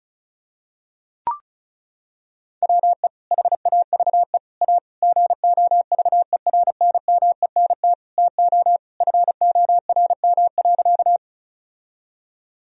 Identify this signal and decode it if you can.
Morse code, heard as a keyed tone, about 34 wpm, 700 Hz. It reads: WE HAVE A GOVERNMENT TO FORM.
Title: Sentences from Top 1000 Words - Review Part 1 - 35wpm